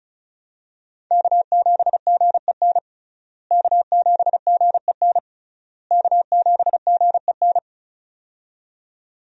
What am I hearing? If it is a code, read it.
Morse code, heard as a keyed tone, about 36 wpm, 700 Hz. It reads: K7GEN K7GEN K7GEN
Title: USA Call Signs - 5 Characters - Repeated 3x - 35wpm